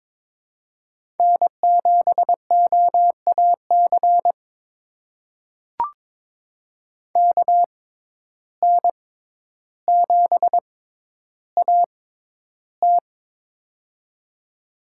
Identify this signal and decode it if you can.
Morse code, heard as a keyed tone, about 22 wpm, spaced as for 11 wpm, 700 Hz. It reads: N7OAC KN7AT
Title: USA Call Signs - 5 Characters - 6x Character Spacing - 22wpm